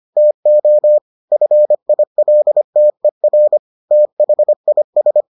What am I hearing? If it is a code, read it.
Morse code, heard as a keyed tone, about 25 wpm, 600 Hz. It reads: TO FILTER THIS